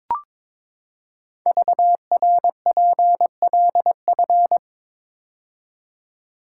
Morse code, heard as a keyed tone, about 22 wpm, 690 Hz. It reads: VRPLF
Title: Code Groups - 5 Letters - 22wpm